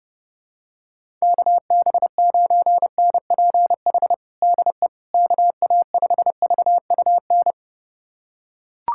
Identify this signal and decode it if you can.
Morse code, heard as a keyed tone, about 31 wpm, 700 Hz. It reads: KB9NPH DE KA5VUN
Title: USA Callsign DE Callsign 30wpm